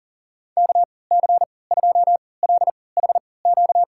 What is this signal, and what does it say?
Morse code, heard as a keyed tone, about 39 wpm, spaced as for 22 wpm, 700 Hz. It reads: KC2LHQ
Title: USA Call Signs - 6 Characters - 3x Character Spacing - 40wpm